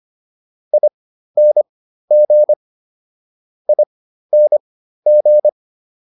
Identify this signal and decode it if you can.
Morse code, heard as a keyed tone, about 25 wpm, spaced as for 13 wpm, 600 Hz. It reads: ING ING